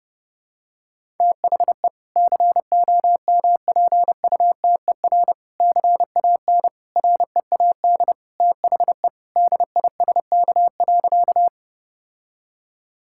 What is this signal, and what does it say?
Morse code, heard as a keyed tone, about 32 wpm, 700 Hz. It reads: THE COMPUTER CAN READ THE DISK.
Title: Sentences from Top 1000 Words - Review Part 6 - 30wpm